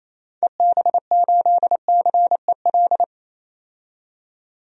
Morse code, heard as a keyed tone, about 28 wpm, 700 Hz. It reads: EB8CEL
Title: DXCC-100 Call Signs - 6 Characters - 28wpm